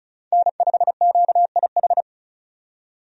Morse code, heard as a keyed tone, about 34 wpm, 700 Hz. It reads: N5QIH
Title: USA Call Signs - 5 Characters - 35wpm